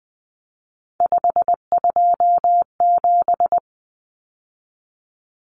Morse code, heard as a keyed tone, about 20 wpm, 700 Hz. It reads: 527
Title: Numbers - 3-Digit 20wpm